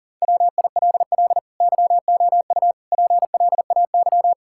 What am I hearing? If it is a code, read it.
Morse code, heard as a keyed tone, about 37 wpm, 700 Hz. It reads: WILL YOU PLAY
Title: Sentences from Top 1000 Words - Review Part 6 - 40wpm